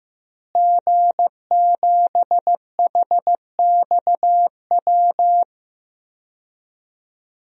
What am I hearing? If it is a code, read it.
Morse code, heard as a keyed tone, about 15 wpm, 700 Hz. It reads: G7HXW
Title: DXCC-100 Call Signs - 5 Characters - Repeated 2x - 15wpm